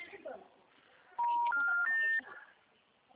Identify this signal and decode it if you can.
Special information tone, with speech.
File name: out- -2011- - - .29913